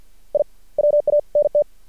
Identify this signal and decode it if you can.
Morse code, heard as a keyed tone, about 50 wpm, spaced as for 44 wpm, 590 Hz. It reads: I WANT
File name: archived / CRSnet